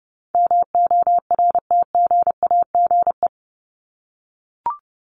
Morse code, heard as a keyed tone, about 30 wpm, 700 Hz. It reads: MORTGAGE
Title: Top 1751-2000 Words 30wpm